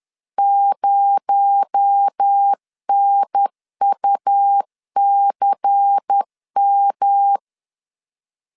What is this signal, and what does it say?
Morse code, heard as a keyed tone, about 10 wpm, 800 Hz. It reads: 0NUCM